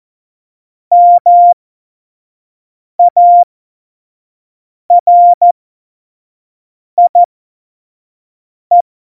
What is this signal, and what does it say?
Morse code, heard as a keyed tone, about 14 wpm, spaced as for 5 wpm, 700 Hz. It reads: MARIE
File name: Morsekurs / Witze / Witz 06. Oktober 2020, 25 F70